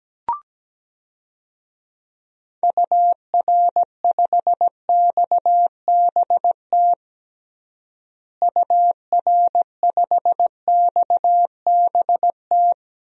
Morse code, heard as a keyed tone, about 17 wpm, 700 Hz. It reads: UR5XBT UR5XBT
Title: DXCC-100 Call Signs - 6 Characters - Repeated 3x - 17wpm